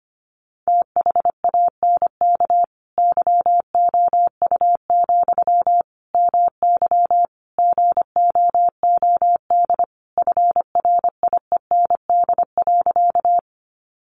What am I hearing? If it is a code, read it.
Morse code, heard as a keyed tone, about 25 wpm, 700 Hz. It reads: THANK YOU, MY GOOD FRIEND.
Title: Sentences from Top 600 Words 25wpm